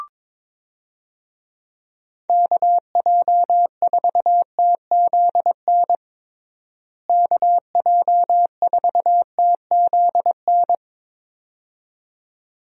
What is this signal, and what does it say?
Morse code, heard as a keyed tone, about 21 wpm, 700 Hz. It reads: KJ4TZN KJ4TZN
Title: USA Call Signs - 6 Characters - Repeated 2x - 22wpm